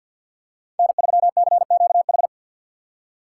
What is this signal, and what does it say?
Morse code, heard as a keyed tone, about 53 wpm, 690 Hz. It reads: N3CQH
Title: USA Call Signs - 6x Character Spacing - 50wpm